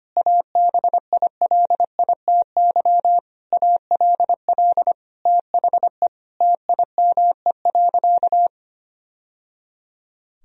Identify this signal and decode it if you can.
Morse code, heard as a keyed tone, about 25 wpm, 700 Hz. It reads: ABILITY ALL THE TIME.
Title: Sentences from Top 1000 Words - Review Part 3 - 25wpm